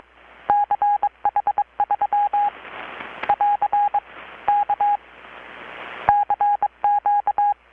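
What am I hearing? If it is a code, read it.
Morse code, heard as a keyed tone, about 22 wpm, 820 Hz. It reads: CH3 <AR> K CQ
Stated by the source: Morse from KPH's Night of Nights XX maritime radiotelegraphy event, received in Chiapas, Mexico about 3,800 km away on a Kenwood R-600 with a grounded 10 m long-wire antenna.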